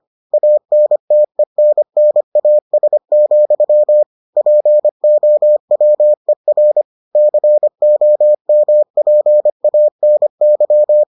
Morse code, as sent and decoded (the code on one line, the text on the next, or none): .- -. - . -. -. .- ... --..-- / .--. --- .-- . .-. / -.-. --- -- .--. .- -. -.--
ANTENNAS, POWER COMPANY